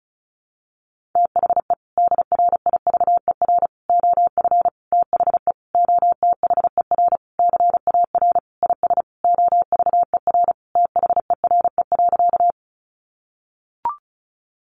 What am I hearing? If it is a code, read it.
Morse code, heard as a keyed tone, about 34 wpm, 700 Hz. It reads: THE DRIVER OF THE OTHER CAR IS OVER THERE.
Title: Sentences from Top 1000 Words 35wpm